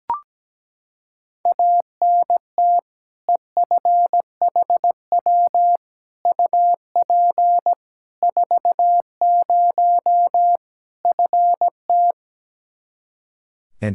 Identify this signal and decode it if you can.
Morse code, heard as a keyed tone, about 17 wpm, 700 Hz. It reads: ANT EFHW UP 40 FT
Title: K1USN SST QSO Practice 17wpm